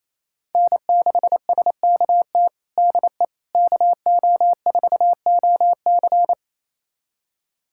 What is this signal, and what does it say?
Morse code, heard as a keyed tone, about 27 wpm, 700 Hz. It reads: N6SKT DE KO4OC
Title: USA Callsign DE Callsign - 5 Characters - 28wpm